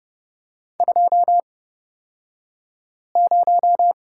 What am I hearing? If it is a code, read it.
Morse code, heard as a keyed tone, about 30 wpm, spaced as for 5 wpm, 700 Hz. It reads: 20
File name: Lesson 38d - Letter q - Two Characters 30wpm